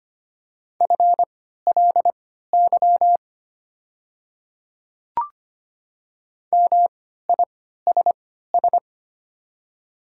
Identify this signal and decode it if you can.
Morse code, heard as a keyed tone, about 25 wpm, spaced as for 14 wpm, 700 Hz. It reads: FLY MISS
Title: 3x Character Spacing - Top 500 Words - 25wpm